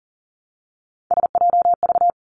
Morse code, heard as a keyed tone, about 42 wpm, 700 Hz. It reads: SJV
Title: ARRL Sections 40wpm